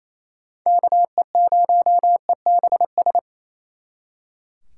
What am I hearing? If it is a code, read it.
Morse code, heard as a keyed tone, about 36 wpm, 700 Hz. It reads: KE0EBS